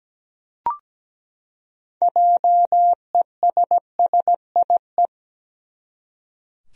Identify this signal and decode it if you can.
Morse code, heard as a keyed tone, about 17 wpm, 700 Hz. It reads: JESSIE
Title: USA Common Names - 17wpm